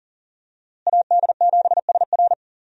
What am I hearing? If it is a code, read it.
Morse code, heard as a keyed tone, about 40 wpm, 700 Hz. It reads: AD7SR
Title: USA Call Signs - 5 Characters - Repeated 3x - 40wpm